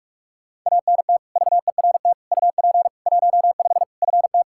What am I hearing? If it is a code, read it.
Morse code, heard as a keyed tone, about 42 wpm, 700 Hz. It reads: ANT VERT UP 15 FT
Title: Practice QSO - Exchange 2 - Info - 45wpm